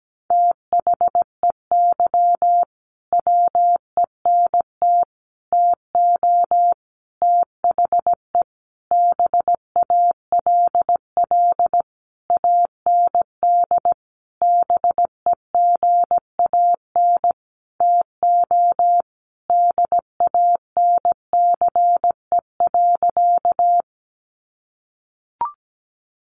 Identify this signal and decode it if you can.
Morse code, heard as a keyed tone, about 17 wpm, 700 Hz. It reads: THEY WENT TO THE BALL AND BEGAN TO DANCE.
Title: Sentences from Top 900 Words 17wpm